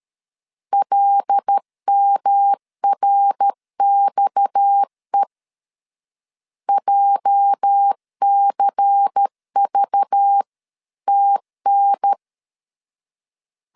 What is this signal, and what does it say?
Morse code, heard as a keyed tone, about 13 wpm, 800 Hz. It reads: LMRXE JCVTN
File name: LEZIONI / 8 YJ45 / 8 YJ 12wpm 2235